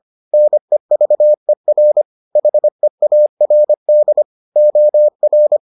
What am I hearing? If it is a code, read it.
Morse code, heard as a keyed tone, about 25 wpm, 600 Hz. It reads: NEVER HEARD OR